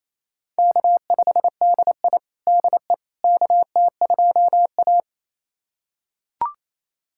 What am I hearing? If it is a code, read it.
Morse code, heard as a keyed tone, about 27 wpm, 700 Hz. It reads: K5DI DE KT2A
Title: USA Callsign DE Callsign - 4 Characters - 28wpm